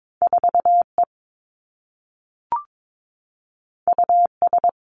Morse code, heard as a keyed tone, about 22 wpm, 700 Hz. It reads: 4 E U S
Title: Two Challenging Letter-Number Combinations 22wpm